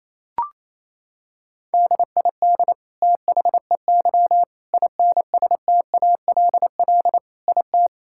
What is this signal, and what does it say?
Morse code, heard as a keyed tone, about 27 wpm, 700 Hz. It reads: DID THEY INSTALL IT
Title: Sentences from Top 1600 Words 28wpm